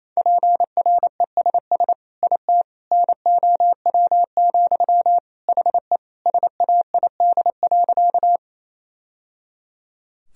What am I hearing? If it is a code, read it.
Morse code, heard as a keyed tone, about 28 wpm, 700 Hz. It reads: PRESS IT NOW, HE SAID.
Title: Sentences from Top 1000 Words - Review Part 1 - 28wpm